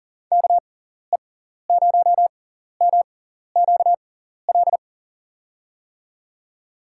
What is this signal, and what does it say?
Morse code, heard as a keyed tone, about 40 wpm, spaced as for 14 wpm, 700 Hz. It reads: KE0MQL
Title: USA Call Signs - 6 Characters - 6x Character Spacing - 40wpm